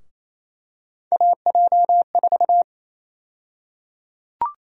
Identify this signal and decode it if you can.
Morse code, heard as a keyed tone, about 28 wpm, 700 Hz. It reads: AJ4
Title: USA Call Sign Prefixes - 28wpm